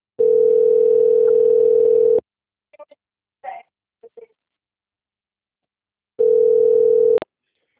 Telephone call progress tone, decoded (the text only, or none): ringback tone